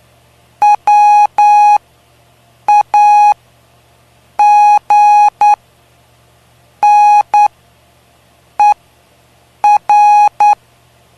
Morse code, heard as a keyed tone, about 9 wpm, spaced as for 6 wpm, 830 Hz. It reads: WAGNER
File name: cw / py2lcd / 05-PY2LCD-Li��o 5